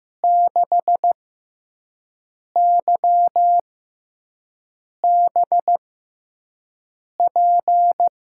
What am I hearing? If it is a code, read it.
Morse code, heard as a keyed tone, about 15 wpm, spaced as for 5 wpm, 700 Hz. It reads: 6YBP
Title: USA Call Signs - 6x Character Spacing - 15wpm